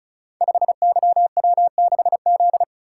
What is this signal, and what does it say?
Morse code, heard as a keyed tone, about 36 wpm, 700 Hz. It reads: 5YW6Z
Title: Code Groups - 5 Letter:Numbers - 35wpm